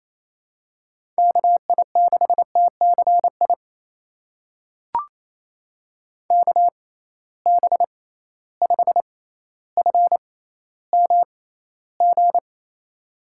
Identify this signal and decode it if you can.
Morse code, heard as a keyed tone, about 28 wpm, spaced as for 14 wpm, 700 Hz. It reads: KI6TCI KB5FMG
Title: USA Call Signs - 6x Character Spacing - 28wpm